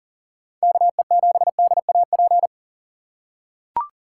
Morse code, heard as a keyed tone, about 40 wpm, 700 Hz. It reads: KE7DAP
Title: USA Call Signs - 4x Character Spacing - 40wpm